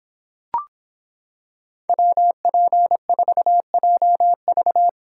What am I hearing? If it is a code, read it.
Morse code, heard as a keyed tone, about 25 wpm, 700 Hz. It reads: WP4JV